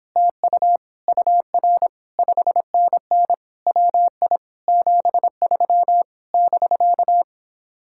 Morse code, heard as a keyed tone, about 29 wpm, 700 Hz. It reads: TU UR 5NN WI 73 <BK>